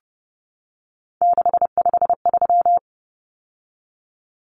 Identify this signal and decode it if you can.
Morse code, heard as a keyed tone, about 30 wpm, 700 Hz. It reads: 653
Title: Numbers - 3-Digit 30wpm